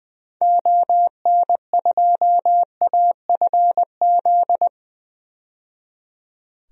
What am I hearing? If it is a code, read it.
Morse code, heard as a keyed tone, about 20 wpm, 700 Hz. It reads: ON2AFZ
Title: DXCC-100 Call Signs - 6 Characters - 20wpm